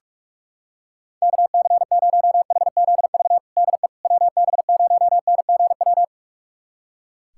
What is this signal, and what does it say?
Morse code, heard as a keyed tone, about 46 wpm, 700 Hz. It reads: KC0HZV DE WB0NGW
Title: USA Callsign DE Callsign - 6 Characters - 45wpm